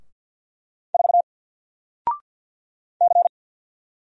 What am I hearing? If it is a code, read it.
Morse code, heard as a keyed tone, about 50 wpm, 700 Hz. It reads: 4 C
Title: Single Character (Letter:Number:Punctuation:Prosign) 50wpm